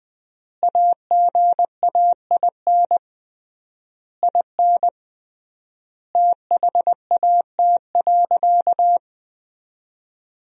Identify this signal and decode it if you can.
Morse code, heard as a keyed tone, about 20 wpm, 700 Hz. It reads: AGAIN IN THAT.
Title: 3x Word-Spacing - Sentences from Top 200 Words - 20wpm